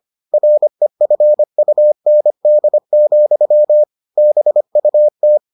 Morse code, heard as a keyed tone, about 25 wpm, 600 Hz. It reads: REFUND, BUT